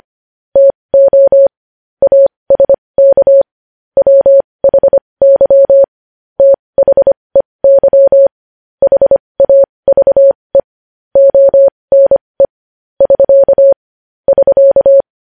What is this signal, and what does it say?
Morse code, heard as a keyed tone, about 25 wpm, spaced as for 20 wpm, 550 Hz. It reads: TO ASK WHY THEY HAVE ONE <SK> <SK>
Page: news-headlines-in-morse-code-at-20-wpm